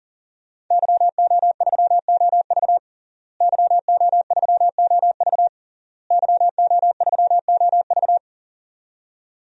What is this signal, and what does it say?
Morse code, heard as a keyed tone, about 40 wpm, 700 Hz. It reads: YO3OV YO3OV YO3OV
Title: DXCC-100 Call Signs - 5 Characters - Repeated 3x - 40wpm